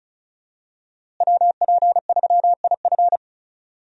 Morse code, heard as a keyed tone, about 35 wpm, 700 Hz. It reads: WP3IF
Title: USA Call Signs - 5 Characters - Repeated 2x - 35wpm